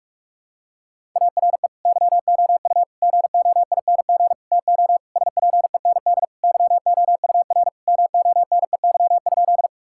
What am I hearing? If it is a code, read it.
Morse code, heard as a keyed tone, about 42 wpm, 700 Hz. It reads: ARE YOU GOING TO SPEND YOUR MONEY?
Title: Sentences from Top 1000 Words - Review Part 5 - 45wpm